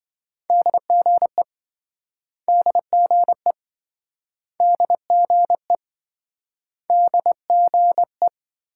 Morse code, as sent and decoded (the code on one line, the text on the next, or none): -.. --. / -.. --. / -.. --. / -.. --. .
DGE DGE DGE DGE